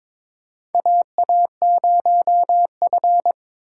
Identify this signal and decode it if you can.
Morse code, heard as a keyed tone, about 22 wpm, 700 Hz. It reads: AA0F
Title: USA Call Signs - 4 Characters - 2x Character Spacing - 22wpm